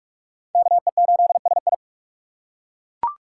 Morse code, heard as a keyed tone, about 38 wpm, 700 Hz. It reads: KE8SI